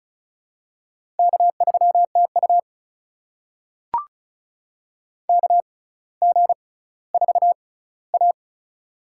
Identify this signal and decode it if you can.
Morse code, heard as a keyed tone, about 36 wpm, spaced as for 18 wpm, 700 Hz. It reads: K3TU KG4A
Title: USA Call Signs - 4 Characters - 6x Character Spacing - 35wpm